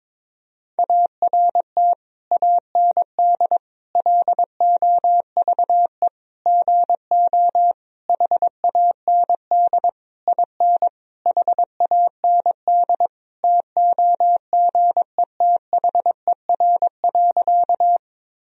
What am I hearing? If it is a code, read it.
Morse code, heard as a keyed tone, about 21 wpm, 700 Hz. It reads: ART AND LOVE GO HAND IN HAND TOGETHER.
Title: Sentences from Top 1000 Words - Review Part 3 - 22wpm